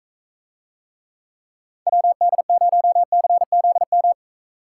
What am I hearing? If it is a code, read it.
Morse code, heard as a keyed tone, about 43 wpm, 700 Hz. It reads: WD0CZM